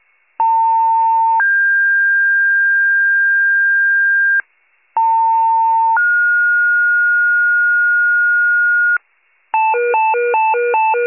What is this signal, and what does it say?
Two-tone page, heard as a one-second tone, then a three-second tone, about 900 then 1600 Hz, about 900 then 1430 Hz, with noise. It is heard from 0.4 to 9.0 s.